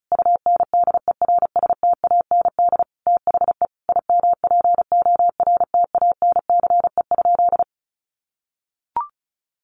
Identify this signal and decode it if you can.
Morse code, heard as a keyed tone, about 34 wpm, 700 Hz. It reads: UNDERSTAND THE IMPORTANCE?
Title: Sentences from Top 1200 Words 35wpm